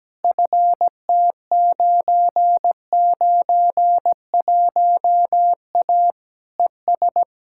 Morse code, heard as a keyed tone, about 17 wpm, 700 Hz. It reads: FT991A ES
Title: Practice QSO - Exchange 2 - Info - 17wpm